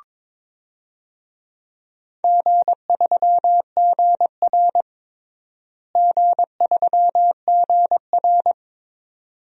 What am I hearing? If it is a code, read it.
Morse code, heard as a keyed tone, about 22 wpm, 700 Hz. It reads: G3GR G3GR
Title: DXCC-100 Call Signs - 4 Characters - Repeated 2x - 22wpm